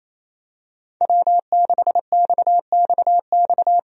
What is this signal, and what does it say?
Morse code, heard as a keyed tone, about 27 wpm, 700 Hz. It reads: W6XXX
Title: USA Call Signs - 5 Characters - 28wpm